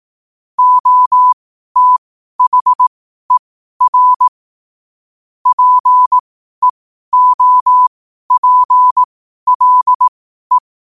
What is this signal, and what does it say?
Morse code, heard as a keyed tone, about 18 wpm, spaced as for 12 wpm, 1000 Hz. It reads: OTHER PEOPLE
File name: qotdfiles / 12wpm